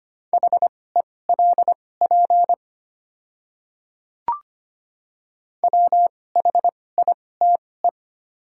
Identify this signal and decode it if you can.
Morse code, heard as a keyed tone, about 25 wpm, spaced as for 18 wpm, 700 Hz. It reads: HELP WHITE
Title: 2x Character Spacing - Top 500 Words - 25wpm